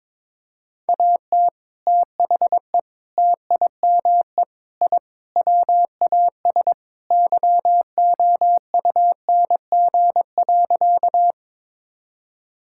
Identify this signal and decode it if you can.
Morse code, heard as a keyed tone, about 22 wpm, 700 Hz. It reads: AT THE TIME I WAS YOUNG.
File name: practice / 22wpm / Sentences with 3-Word Phrases - 22wpm